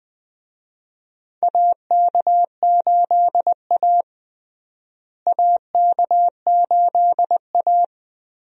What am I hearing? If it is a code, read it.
Morse code, heard as a keyed tone, about 20 wpm, 700 Hz. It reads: AK8A AK8A
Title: USA Call Signs - 4 Characters - Repeated 3x - 20wpm